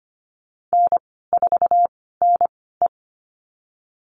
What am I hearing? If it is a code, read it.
Morse code, heard as a keyed tone, about 25 wpm, spaced as for 16 wpm, 700 Hz. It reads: N4NE
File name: Lesson 15g - Number 5 - Call Signs 25wpm